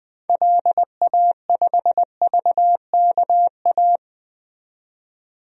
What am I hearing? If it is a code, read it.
Morse code, heard as a keyed tone, about 20 wpm, 700 Hz. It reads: LA5VKA
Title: DXCC-100 Call Signs - 6 Characters - 20wpm